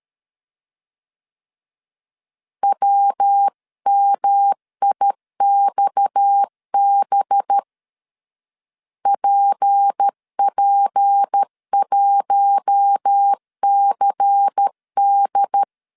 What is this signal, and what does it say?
Morse code, heard as a keyed tone, about 13 wpm, 800 Hz. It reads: WMIXB PP1CD
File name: CW / DET / 1 6 1500C S A 12wpm 3440